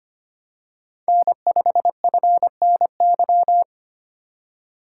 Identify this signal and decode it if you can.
Morse code, heard as a keyed tone, about 25 wpm, 700 Hz. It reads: N5FNY
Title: USA Call Signs - 3x Character Spacing - 25wpm